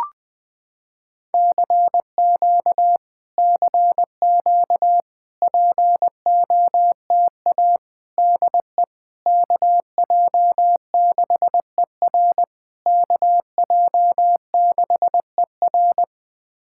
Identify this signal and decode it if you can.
Morse code, heard as a keyed tone, about 20 wpm, 700 Hz. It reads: CQ CQ POTA DE KJ6ER KJ6ER
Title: POTA QSO Practice - 20wpm